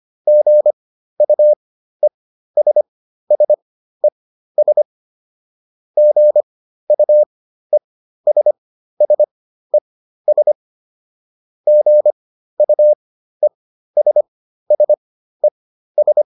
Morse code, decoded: GUESSES GUESSES GUESSES

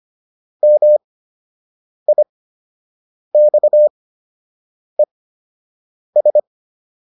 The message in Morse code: MIXES

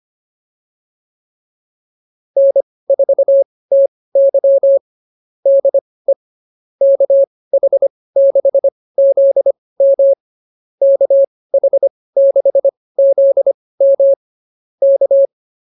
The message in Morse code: N4TY DE KH6ZM KH6ZM K

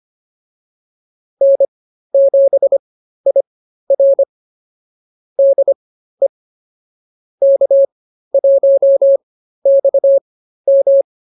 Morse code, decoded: N7IR DE K1XM

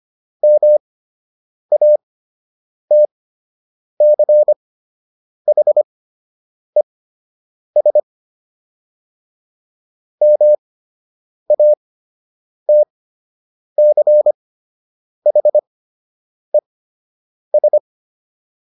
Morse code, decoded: MATCHES MATCHES

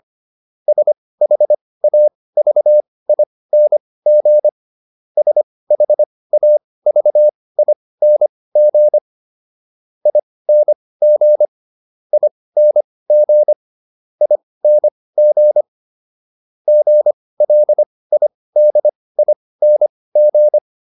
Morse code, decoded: SHAVING SHAVING ING ING ING GLIDING

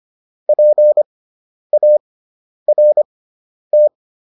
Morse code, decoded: PART

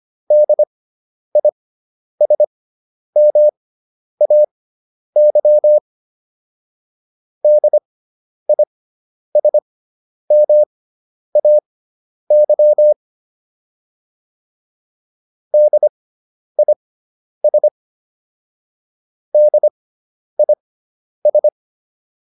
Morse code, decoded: DISMAY DISMAY DIS DIS

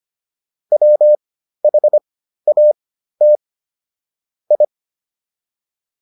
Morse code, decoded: WHAT I

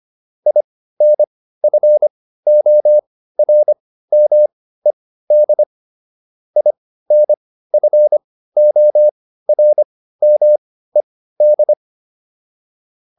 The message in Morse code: INFORMED INFORMED

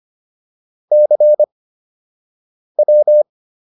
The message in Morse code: CW

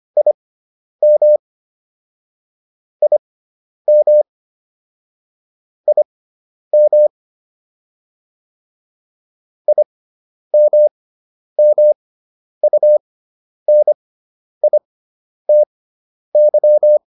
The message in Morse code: IM IM IM IMMUNITY

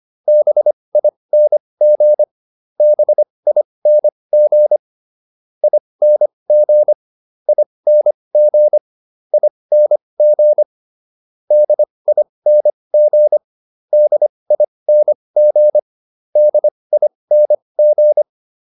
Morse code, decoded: BING BING ING ING ING DING DING DING

